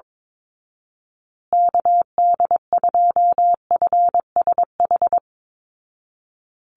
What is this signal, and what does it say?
Morse code, heard as a keyed tone, about 22 wpm, 700 Hz. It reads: KD2FSH
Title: Call Signs Repeated 2x - 22wpm